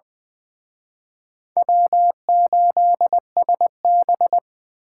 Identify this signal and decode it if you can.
Morse code, heard as a keyed tone, about 23 wpm, 700 Hz. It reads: W8SB
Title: Speed Racing - USA 1x2 Call Signs 20wpm